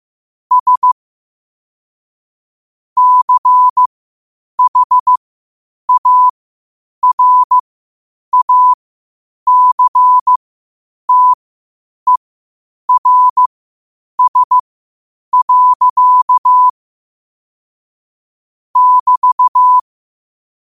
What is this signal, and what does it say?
Morse code, heard as a keyed tone, about 15 wpm, spaced as for 8 wpm, 1000 Hz. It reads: S CHARACTERS. <BT>
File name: qotdfiles / 8wpm